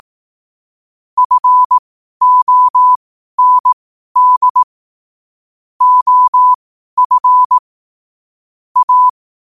Morse code, heard as a keyed tone, about 18 wpm, spaced as for 12 wpm, 1000 Hz. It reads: FOND OF A
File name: qotdfiles / 12wpm